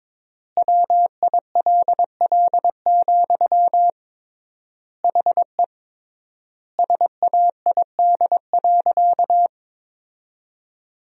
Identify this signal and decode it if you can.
Morse code, heard as a keyed tone, about 22 wpm, 700 Hz. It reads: WILL, HE SAID.
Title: 3x Word-Spacing - Sentences from Top 200 Words - 22wpm